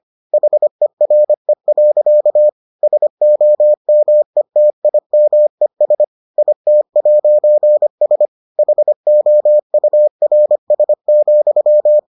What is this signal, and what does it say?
Morse code, heard as a keyed tone, about 25 wpm, 600 Hz. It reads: HERE. SOMETIMES IT'S HOURS,